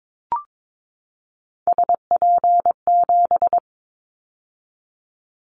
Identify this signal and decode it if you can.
Morse code, heard as a keyed tone, about 22 wpm, 700 Hz. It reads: SP7
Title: Callsign Prefixes for Top 25 DXCC - 22wpm